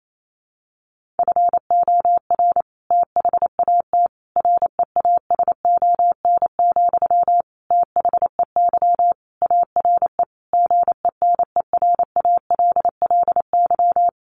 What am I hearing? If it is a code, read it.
Morse code, heard as a keyed tone, about 28 wpm, 700 Hz. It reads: FOR THAT REASON, THEY ARE GENERALLY
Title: Sentences from Top 1500 Words 28wpm